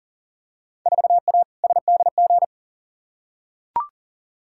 Morse code, heard as a keyed tone, about 40 wpm, 700 Hz. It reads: 4A SDG
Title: Field Day Exchanges - 40wpm